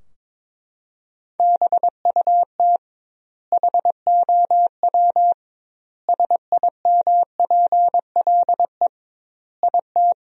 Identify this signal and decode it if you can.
Morse code, heard as a keyed tone, about 22 wpm, 700 Hz. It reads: BUT HOW SIMPLE IT
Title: 2x Word-Spacing - Sentences from Top 400 Words - 22wpm